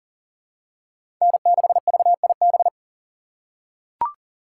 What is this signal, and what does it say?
Morse code, heard as a keyed tone, about 40 wpm, 700 Hz. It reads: N6VIB
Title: USA Call Signs - 6x Character Spacing - 40wpm